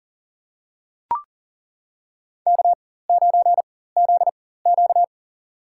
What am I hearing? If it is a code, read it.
Morse code, heard as a keyed tone, about 37 wpm, spaced as for 18 wpm, 700 Hz. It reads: K9ZQ
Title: USA Call Signs - 4 Characters - 4x Character Spacing - 40wpm